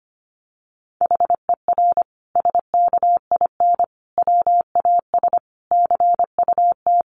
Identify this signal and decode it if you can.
Morse code, heard as a keyed tone, about 25 wpm, 700 Hz. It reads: HER SKIN WAS CUT
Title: Sentences from Top 1300 Words 25wpm